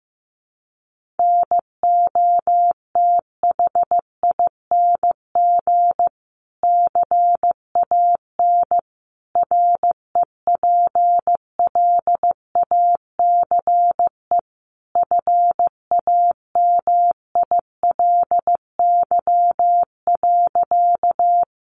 Morse code, heard as a keyed tone, about 15 wpm, 700 Hz. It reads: NOTHING CAN REPLACE FAMILY.
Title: Sentences from Top 1400 Words 15wpm